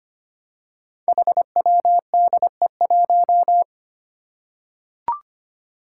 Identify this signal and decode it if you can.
Morse code, heard as a keyed tone, about 25 wpm, 700 Hz. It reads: HWDE1
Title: Five Challenging Letter-Number Combinations 25wpm